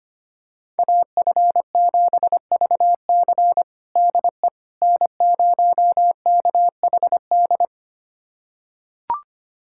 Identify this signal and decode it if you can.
Morse code, heard as a keyed tone, about 25 wpm, 700 Hz. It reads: AF7VC DE N0KHD
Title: USA Callsign DE Callsign - 5 Characters - 25wpm